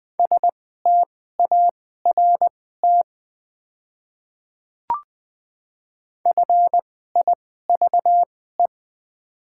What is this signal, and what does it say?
Morse code, heard as a keyed tone, about 20 wpm, spaced as for 15 wpm, 690 Hz. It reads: START FIVE
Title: 2x Character Spacing - Top 500 Words - 20wpm